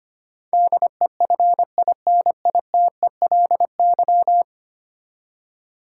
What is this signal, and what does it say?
Morse code, heard as a keyed tone, about 25 wpm, 700 Hz. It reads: DEFINITELY
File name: Ten Letter Words - 25wpm